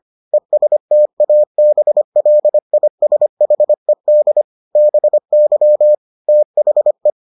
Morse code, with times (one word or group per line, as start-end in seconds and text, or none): 0.3-4.4 ESTABLISHED
4.8-5.9 BY
6.3-7.1 THE